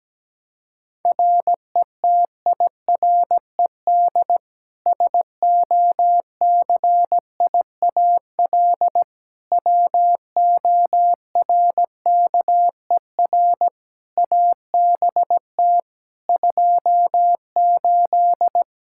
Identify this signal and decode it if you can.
Morse code, heard as a keyed tone, about 17 wpm, 700 Hz. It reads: RETIRED SOCIAL WORKER ABT 28
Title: K1USN SST QSO Practice 17wpm